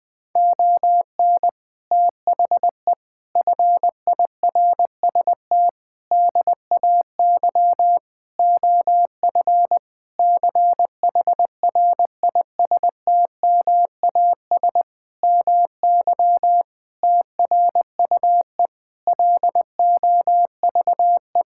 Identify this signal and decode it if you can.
Morse code, heard as a keyed tone, about 20 wpm, 700 Hz. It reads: ON THE FIRST DAY OF CHRISTMAS MY TRUE LOVE
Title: Sentences with 8-Word Phrases - 20wpm